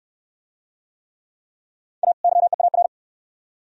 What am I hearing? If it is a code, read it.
Morse code, heard as a keyed tone, about 60 wpm, 700 Hz. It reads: A YEAR